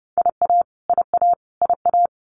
Morse code, heard as a keyed tone, about 30 wpm, 700 Hz. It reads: IA IA IA
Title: US States Repeated 3x 30wpm